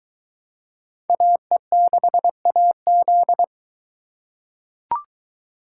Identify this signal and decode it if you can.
Morse code, heard as a keyed tone, about 21 wpm, 700 Hz. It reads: AE6AZ